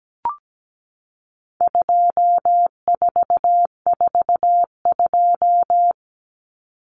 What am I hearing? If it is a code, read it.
Morse code, heard as a keyed tone, about 17 wpm, 700 Hz. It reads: 2442
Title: Numbers - 4-Digit 17wpm